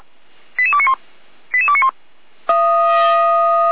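Five-tone selective call (ZVEI), heard as quick runs of five tones, tones of about 70 ms. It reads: 88291 88291
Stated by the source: Five-tone ZVEI selective call received on an SDR.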